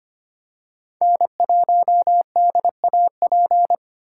Morse code, heard as a keyed tone, about 25 wpm, 700 Hz. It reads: N1DAP
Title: USA Call Signs - 8x Character Spacing - 25wpm